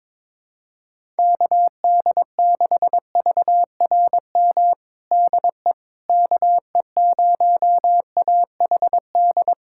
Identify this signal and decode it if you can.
Morse code, heard as a keyed tone, about 22 wpm, 700 Hz. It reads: KD6VRM DE KE0AHD
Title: USA Callsign DE Callsign 22wpm